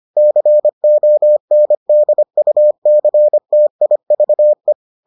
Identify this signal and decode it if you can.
Morse code, heard as a keyed tone, about 25 wpm, 600 Hz. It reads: CONDUCTIVE